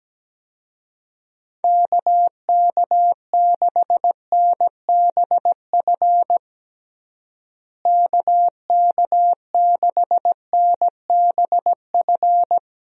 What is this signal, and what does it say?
Morse code, heard as a keyed tone, about 17 wpm, 700 Hz. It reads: KK6NBF KK6NBF
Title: USA Call Signs - 6 Characters - Repeated 2x - 17wpm